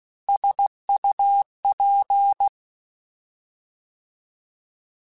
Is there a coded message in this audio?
Yes: Morse code from 0.3 to 2.5 s, heard as a keyed tone, about 16 wpm, 800 Hz. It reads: SUP